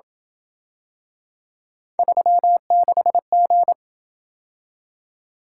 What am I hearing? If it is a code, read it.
Morse code, heard as a keyed tone, about 27 wpm, 700 Hz. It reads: 36G